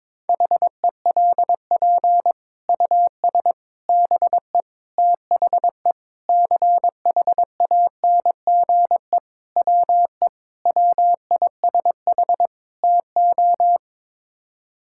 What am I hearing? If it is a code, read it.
Morse code, heard as a keyed tone, about 21 wpm, 700 Hz. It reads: HELP US BE THE CHANGE WE WISH TO
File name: practice / 22wpm / Sets of 8 Words - 22wpm